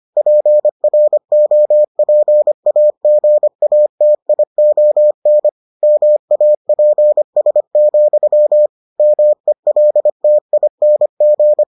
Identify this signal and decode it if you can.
Morse code, heard as a keyed tone, about 25 wpm, 600 Hz. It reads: PROPAGATION MAPS, MELTING